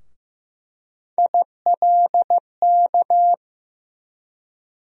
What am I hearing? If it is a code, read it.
Morse code, heard as a keyed tone, about 15 wpm, 700 Hz. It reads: ILK
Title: USA Call Sign Postfixes - 3 Letters 15wpm